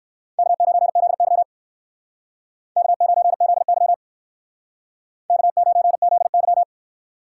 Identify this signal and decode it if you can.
Morse code, heard as a keyed tone, about 60 wpm, 700 Hz. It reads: K9ZY K9ZY K9ZY